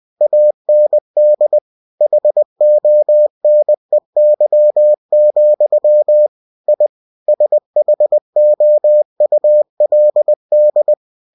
Morse code, heard as a keyed tone, about 20 wpm, 600 Hz. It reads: AND HONEY, I SHOULD
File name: cw-books / winnie-the-pooh / winnie-20wpm-page-0037